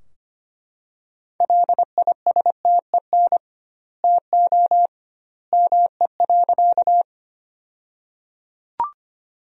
Morse code, heard as a keyed tone, about 25 wpm, 700 Hz. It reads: LISTEN TO ME.
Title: 2x Word-Spacing - Sentences from Top 400 Words - 25wpm